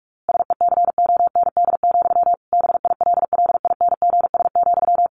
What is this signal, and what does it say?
Morse code, heard as a keyed tone, about 44 wpm, 700 Hz. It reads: SECOND, BILLINGS,
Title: A Sound Of Thunder 45wpm